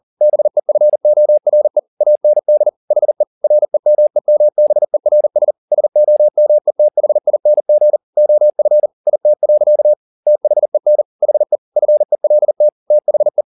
Morse code, heard as a keyed tone, about 40 wpm, 600 Hz. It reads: BEFORE AND HE REMEMBERS SOMETHING OF IT. THEN HE FELT THE